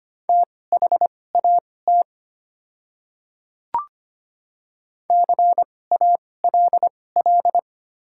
Morse code, heard as a keyed tone, about 25 wpm, spaced as for 18 wpm, 690 Hz. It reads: THAT CALL
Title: 2x Character Spacing - Top 100 Words - 25wpm